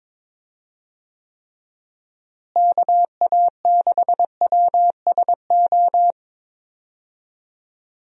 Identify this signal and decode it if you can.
Morse code, heard as a keyed tone, about 22 wpm, 700 Hz. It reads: KA6WSO
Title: USA Call Signs - 6 Characters - Repeated 2x - 22wpm